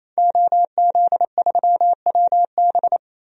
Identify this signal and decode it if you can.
Morse code, heard as a keyed tone, about 28 wpm, 700 Hz. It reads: OZ3WB
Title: DXCC-100 Call Signs - 5 Characters - Repeated 3x - 28wpm